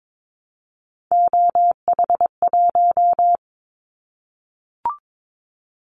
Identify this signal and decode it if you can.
Morse code, heard as a keyed tone, about 22 wpm, 700 Hz. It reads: OH1
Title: Callsign Prefixes for Top 50 DXCC - 22wpm